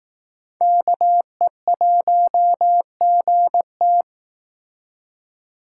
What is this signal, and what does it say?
Morse code, heard as a keyed tone, about 18 wpm, 700 Hz. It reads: KE1GT
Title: Speed Racing - USA 2x2 Call Signs 15wpm